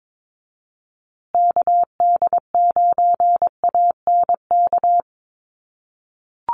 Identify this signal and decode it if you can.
Morse code, heard as a keyed tone, about 22 wpm, 700 Hz. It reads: KD9ANK
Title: USA Call Signs Repeated 3x 22wpm